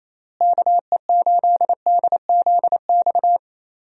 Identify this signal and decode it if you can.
Morse code, heard as a keyed tone, about 28 wpm, 700 Hz. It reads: KE8DZX
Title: USA Call Signs - 6 Characters - Repeated 3x - 28wpm